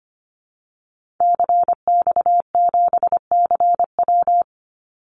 Morse code, heard as a keyed tone, about 25 wpm, 700 Hz. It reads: CX7CW
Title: Call Signs Repeated 3x - 25wpm